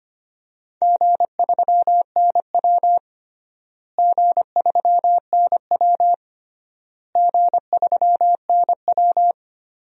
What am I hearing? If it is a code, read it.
Morse code, heard as a keyed tone, about 25 wpm, 700 Hz. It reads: G3NW G3NW G3NW
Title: DXCC-100 Call Signs - 4 Characters - Repeated 3x - 25wpm